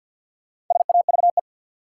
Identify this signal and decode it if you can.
Morse code, heard as a keyed tone, about 50 wpm, 700 Hz. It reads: SAVE
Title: Sentences from Top 1100 Words - New Word Drill - 50wpm